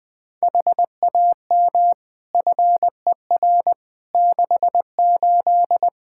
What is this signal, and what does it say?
Morse code, heard as a keyed tone, about 20 wpm, 700 Hz. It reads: HAM FER 68
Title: K1USN SST QSO Practice 20wpm